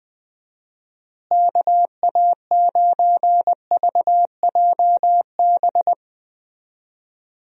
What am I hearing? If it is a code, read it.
Morse code, heard as a keyed tone, about 20 wpm, 700 Hz. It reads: KA9VJB
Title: USA Call Signs - 6 Characters - Repeated 2x - 20wpm